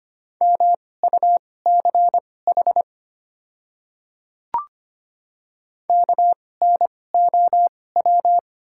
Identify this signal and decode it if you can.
Morse code, heard as a keyed tone, about 25 wpm, spaced as for 18 wpm, 700 Hz. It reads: MUCH KNOW
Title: 2x Character Spacing - Top 300 Words - 25wpm